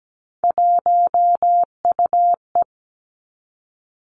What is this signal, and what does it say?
Morse code, heard as a keyed tone, about 17 wpm, 700 Hz. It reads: 1UE
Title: Three Challenging Letter-Number Combinations 17wpm